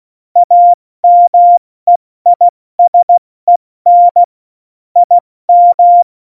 Morse code, heard as a keyed tone, about 16 wpm, spaced as for 14 wpm, 700 Hz. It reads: AMEISEN IM
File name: Morsekurs / Witze / Witz 03.Juli 201, 70 F80